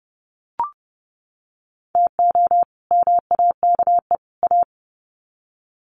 Morse code, heard as a keyed tone, about 30 wpm, 700 Hz. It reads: TO MAKE A
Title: Sets of 3 Words 30wpm